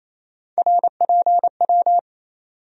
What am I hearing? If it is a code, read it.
Morse code, heard as a keyed tone, about 28 wpm, 700 Hz. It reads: RPW
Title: USA Call Sign Postfixes - 3 Letters 28wpm